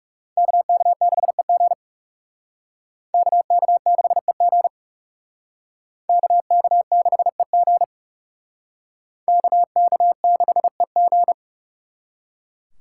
Morse code, heard as a keyed tone, about 37 wpm, 700 Hz. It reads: KK6EG KK6EG KK6EG KK6EG
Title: Speed Racing - USA 2x2 Call Signs 30wpm